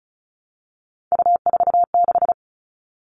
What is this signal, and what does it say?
Morse code, heard as a keyed tone, about 34 wpm, 700 Hz. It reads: U46
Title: All Cut Numbers - 3-Digit 35wpm